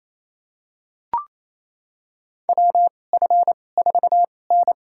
Morse code, heard as a keyed tone, about 28 wpm, spaced as for 20 wpm, 700 Hz. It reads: WF4N
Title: USA Call Signs - 4 Characters - 2x Character Spacing - 28wpm